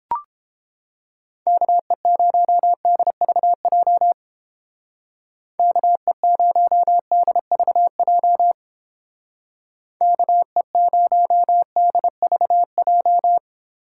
Morse code, heard as a keyed tone, about 31 wpm, 700 Hz. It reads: KE0DVJ KE0DVJ KE0DVJ